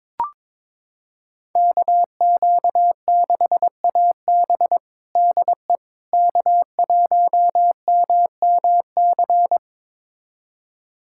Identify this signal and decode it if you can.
Morse code, heard as a keyed tone, about 22 wpm, 700 Hz. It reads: KQ6AB DE K1MMC
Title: USA Callsign DE Callsign - 5 Characters - 22wpm